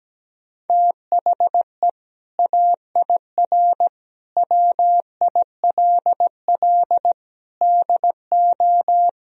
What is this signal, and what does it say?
Morse code, heard as a keyed tone, about 17 wpm, 700 Hz. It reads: THE AIR WILL DO
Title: Sentences from Top 1000 Words - Review Part 1 - 17wpm